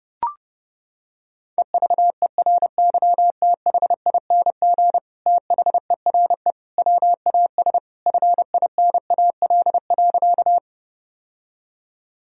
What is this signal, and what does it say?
Morse code, heard as a keyed tone, about 30 wpm, 700 Hz. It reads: EVERYTHING THERE WAS FINAL.
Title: Sentences from Top 1000 Words - Review Part 2 - 30wpm